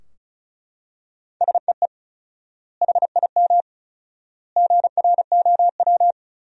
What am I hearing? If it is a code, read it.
Morse code, heard as a keyed tone, about 36 wpm, 700 Hz. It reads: SEE HIM GROW
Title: 4x Word-Spacing - Sentences from Top 300 Words - 35wpm